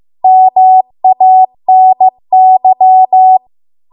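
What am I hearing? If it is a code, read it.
Morse code, heard as a keyed tone, about 15 wpm, 750 Hz. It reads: MANY
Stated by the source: Morse code practice at 15 wpm from the national amateur radio association's code archive.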